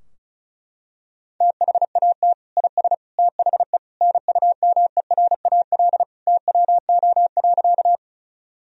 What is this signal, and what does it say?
Morse code, heard as a keyed tone, about 34 wpm, 700 Hz. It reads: THAT IS THE NUMERAL TWO.
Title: Sentences from Top 1000 Words - Review Part 2 - 35wpm